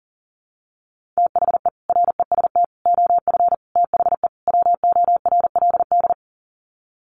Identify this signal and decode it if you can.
Morse code, heard as a keyed tone, about 40 wpm, 700 Hz. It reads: THE REST OF THE WORLD
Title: Sets of 5 Words 40wpm